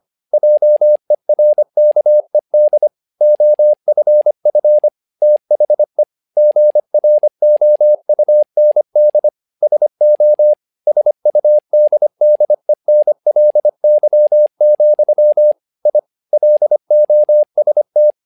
Morse code, heard as a keyed tone, about 25 wpm, 600 Hz. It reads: JERKED OFF THE GROUND SO SUDDENLY, I LOST
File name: learning center / tower trouble / tower trouble 25wpm